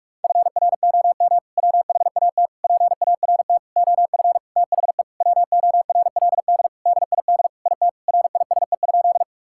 Morse code, heard as a keyed tone, about 46 wpm, 690 Hz. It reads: FROM WHAT PART OF THE WORLD DID IT RISE?
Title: Sentences from Top 1000 Words - Review Part 6 - 45wpm